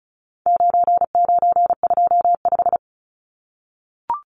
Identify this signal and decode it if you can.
Morse code, heard as a keyed tone, about 34 wpm, 700 Hz. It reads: 9925
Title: Numbers - 4-Digit 35wpm